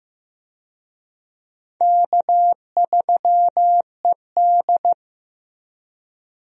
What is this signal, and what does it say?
Morse code, heard as a keyed tone, about 15 wpm, 700 Hz. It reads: K3ED
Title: USA Call Signs - 4 Characters - Repeated 3x - 15wpm